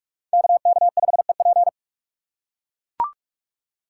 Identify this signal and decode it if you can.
Morse code, heard as a keyed tone, about 44 wpm, 700 Hz. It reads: KK5EP